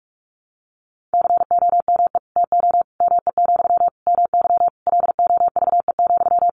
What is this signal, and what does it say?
Morse code, heard as a keyed tone, about 44 wpm, 700 Hz. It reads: COME TO ME, MY LOVE,
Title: Sentences from Top 1500 Words 45wpm